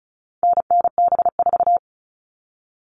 Morse code, heard as a keyed tone, about 34 wpm, 700 Hz. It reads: NNB4
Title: All Cut Numbers - 4-Digit 35wpm